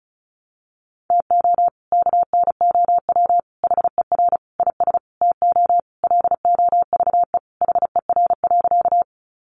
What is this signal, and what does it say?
Morse code, heard as a keyed tone, about 34 wpm, 700 Hz. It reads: TO KNOW HER IS TO LOVE HER.
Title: Sentences from Top 400 Words 35wpm